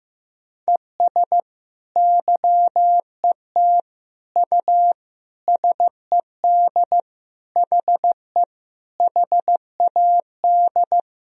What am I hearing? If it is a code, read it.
Morse code, heard as a keyed tone, about 15 wpm, 700 Hz. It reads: ES YET U SED HE HAD
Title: Sentences from Top 500 Words and Abbreviations 15wpm